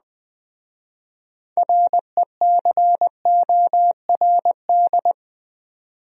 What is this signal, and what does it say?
Morse code, heard as a keyed tone, about 20 wpm, 700 Hz. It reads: RECORD
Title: Speed-Racing Top 500 Words 17wpm